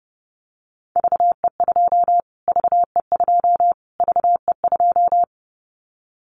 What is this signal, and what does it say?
Morse code, heard as a keyed tone, about 30 wpm, 700 Hz. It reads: VE2 VE2 VE2
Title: Callsign Prefixes for Top 25 DXCC - Repeated 3x - 30wpm